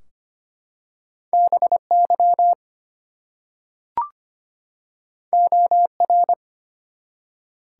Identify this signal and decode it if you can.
Morse code, heard as a keyed tone, about 25 wpm, 700 Hz. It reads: BY OR
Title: Two Letter Words - 25wpm